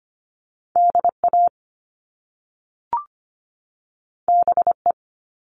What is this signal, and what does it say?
Morse code, heard as a keyed tone, about 25 wpm, 700 Hz. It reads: DA BE
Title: Top Two Letter Combinations in Words 25wpm